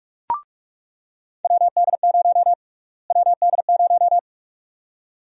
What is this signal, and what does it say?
Morse code, heard as a keyed tone, about 46 wpm, 700 Hz. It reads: WD0 WD0
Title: USA Call Sign Prefixes - Repeated 2x - 45wpm